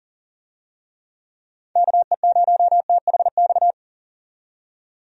Morse code, heard as a keyed tone, about 40 wpm, 700 Hz. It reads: KE0THX
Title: USA Call Signs - 6 Characters - 40wpm